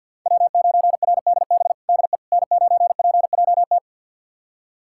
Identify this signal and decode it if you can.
Morse code, heard as a keyed tone, about 52 wpm, 700 Hz. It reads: W9RDB DE N9PJT
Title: USA Callsign DE Callsign - 5 Characters - 50wpm